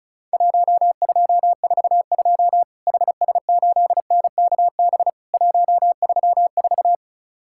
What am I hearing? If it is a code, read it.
Morse code, heard as a keyed tone, about 36 wpm, 700 Hz. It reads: 1242 HS8NKB 134
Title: Worked All Europe DX Contest - QTC Exchange - 35wpm